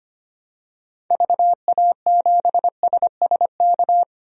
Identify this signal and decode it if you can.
Morse code, heard as a keyed tone, about 25 wpm, 700 Hz. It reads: VA7SSK
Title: DXCC-100 Call Signs - 6 Characters - 25wpm